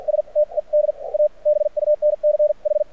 Morse code, heard as a keyed tone, about 50 wpm, spaced as for 40 wpm, 600 Hz. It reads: STED A BUNCH
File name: archived / CRSnet